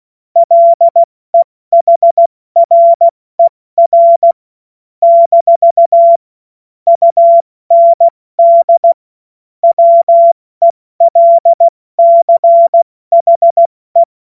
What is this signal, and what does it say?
Morse code, heard as a keyed tone, about 16 wpm, 680 Hz. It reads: LEHRER - UND WELCHE
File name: Morsekurs / Witze / Witz 10. August 2021, 70 F80